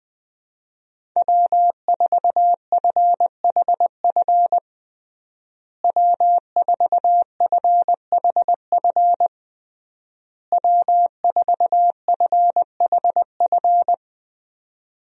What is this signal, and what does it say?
Morse code, heard as a keyed tone, about 21 wpm, 700 Hz. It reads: W4FHF W4FHF W4FHF
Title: USA Call Signs - 5 Characters - Repeated 3x - 20wpm